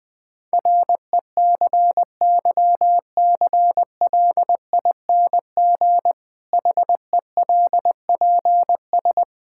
Morse code, heard as a keyed tone, about 20 wpm, 700 Hz. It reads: RECYCLING HELPS
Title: Difficult Sentences with Hints - 20wpm